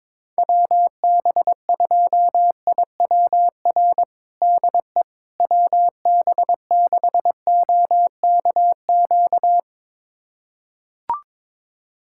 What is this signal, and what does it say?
Morse code, heard as a keyed tone, about 22 wpm, 700 Hz. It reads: WB2IWR DE WB6OKQ
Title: USA Callsign DE Callsign 22wpm